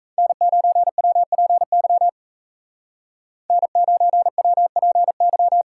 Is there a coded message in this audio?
Yes: Morse code from 0.2 to 5.6 s, heard as a keyed tone, about 36 wpm, 700 Hz. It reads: N9WPY N9WPY